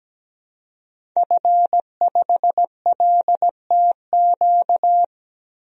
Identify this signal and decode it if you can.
Morse code, heard as a keyed tone, about 17 wpm, 700 Hz. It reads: F5LTQ
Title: DXCC-100 Call Signs - 5 Characters - 17wpm